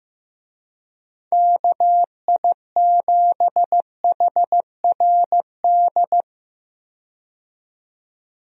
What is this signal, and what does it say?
Morse code, heard as a keyed tone, about 15 wpm, 700 Hz. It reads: KI7HRD
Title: USA Call Signs - 6 Characters - 15wpm